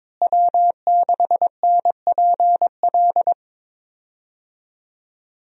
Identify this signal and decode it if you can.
Morse code, heard as a keyed tone, about 22 wpm, 700 Hz. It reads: W6NPL
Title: USA Call Signs - 5 Characters - Repeated 3x - 22wpm